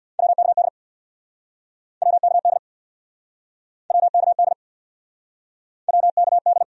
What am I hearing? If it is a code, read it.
Morse code, heard as a keyed tone, about 69 wpm, 700 Hz. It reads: WKD WKD WKD WKD